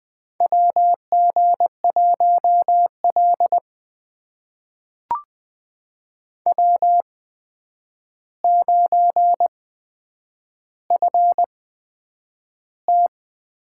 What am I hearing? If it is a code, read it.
Morse code, heard as a keyed tone, about 20 wpm, spaced as for 9 wpm, 700 Hz. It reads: WG1L W9FT
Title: USA Call Signs - 4 Characters - 8x Character Spacing - 20wpm